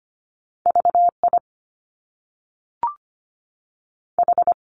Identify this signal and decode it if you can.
Morse code, heard as a keyed tone, about 25 wpm, 700 Hz. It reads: V I H
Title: Top Two Letter Combinations in Words 25wpm